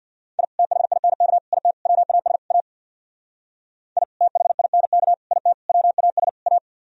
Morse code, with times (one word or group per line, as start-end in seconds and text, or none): 0.4-0.4 I
0.6-1.4 THINK
1.5-1.7 IT
1.9-2.4 WAS
2.5-2.6 A
4.0-4.0 I
4.2-5.1 THINK
5.3-5.5 IT
5.7-6.3 WAS
6.5-6.6 A